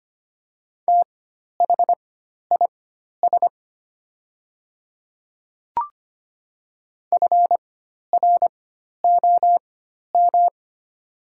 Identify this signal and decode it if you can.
Morse code, heard as a keyed tone, about 25 wpm, spaced as for 12 wpm, 700 Hz. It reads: THIS FROM
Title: 4x Character Spacing - Top 500 Words - 25wpm